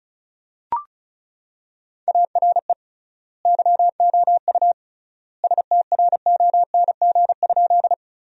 Morse code, heard as a keyed tone, about 34 wpm, 700 Hz. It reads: ARE YOU STRONG?
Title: 3x Word-Spacing - Sentences from Top 500 Words - 35wpm